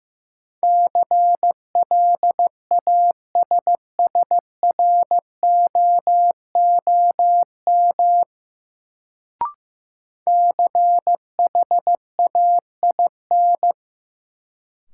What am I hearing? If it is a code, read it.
Morse code, heard as a keyed tone, about 15 wpm, 700 Hz. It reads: CLASSROOM CHAIN
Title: Sentences from Top 1500 Words - New Word Drill - 15wpm